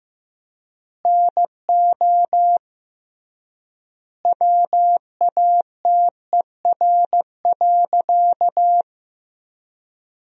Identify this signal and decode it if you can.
Morse code, heard as a keyed tone, about 16 wpm, 700 Hz. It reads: NO WATER.
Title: 3x Word-Spacing - Sentences from Top 100 Words - 15wpm